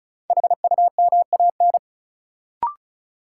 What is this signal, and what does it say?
Morse code, heard as a keyed tone, about 36 wpm, 700 Hz. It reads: HUMAN
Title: Sentences from Top 600 Words - New Word Drill - 35wpm